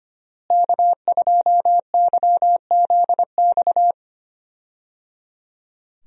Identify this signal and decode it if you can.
Morse code, heard as a keyed tone, about 25 wpm, 700 Hz. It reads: K2YZX
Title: USA Call Signs - 5 Characters - Repeated 3x - 25wpm